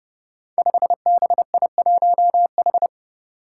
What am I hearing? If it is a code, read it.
Morse code, heard as a keyed tone, about 28 wpm, 700 Hz. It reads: 5BI1H
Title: Five Challenging Letter-Number Combinations 30wpm